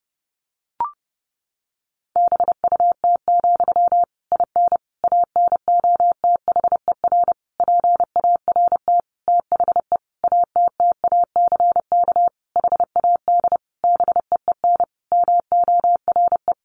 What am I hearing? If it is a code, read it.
Morse code, heard as a keyed tone, about 28 wpm, 700 Hz. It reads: BUT, IN ANOTHER PART THE ATTACK HAD BEEN MORE
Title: Sentences from Top 1500 Words 30wpm